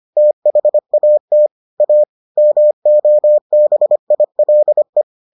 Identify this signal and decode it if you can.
Morse code, heard as a keyed tone, about 25 wpm, 600 Hz. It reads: THAT A MOBILE